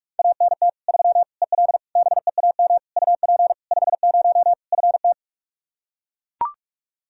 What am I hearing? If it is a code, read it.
Morse code, heard as a keyed tone, about 41 wpm, 700 Hz. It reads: ANT 3 EL BEAM UP 50 FT
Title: Practice QSO - Exchange 2 - Info - 45wpm